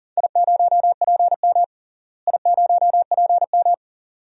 Morse code, heard as a keyed tone, about 40 wpm, 700 Hz. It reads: I0PM I0PM
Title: DXCC-100 Call Signs - 4 Characters - Repeated 2x - 40wpm